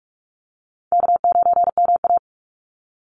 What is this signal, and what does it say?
Morse code, heard as a keyed tone, about 44 wpm, 700 Hz. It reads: K9MA
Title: Call Signs Repeated 3x - 45wpm